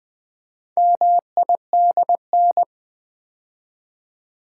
Morse code, heard as a keyed tone, about 20 wpm, 700 Hz. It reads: MIDN
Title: Build A Word - 2 Letters At A Time - 8-Letter Words - 20wpm